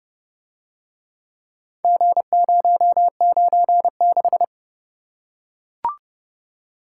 Morse code, heard as a keyed tone, about 31 wpm, 700 Hz. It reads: G096